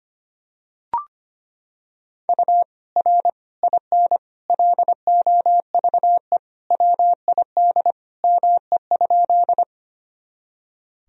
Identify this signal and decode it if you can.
Morse code, heard as a keyed tone, about 25 wpm, 700 Hz. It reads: U R IN LOVE WID ME?
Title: Sentences from Top 400 Words and Abbreviations 25wpm